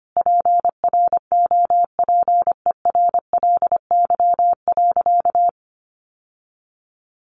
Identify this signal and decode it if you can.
Morse code, heard as a keyed tone, about 25 wpm, 700 Hz. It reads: PROPERLY.
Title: Sentences from Top 1100 Words 25wpm